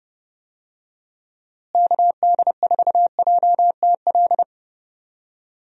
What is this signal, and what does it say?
Morse code, heard as a keyed tone, about 30 wpm, 700 Hz. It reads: KD4JTL
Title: USA Call Signs - 6 Characters - Repeated 2x - 30wpm